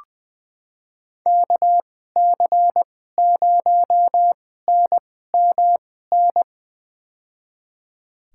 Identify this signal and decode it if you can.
Morse code, heard as a keyed tone, about 21 wpm, spaced as for 15 wpm, 700 Hz. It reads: KC0NMN
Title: USA Call Signs - 6 Characters - 2x Character Spacing - 20wpm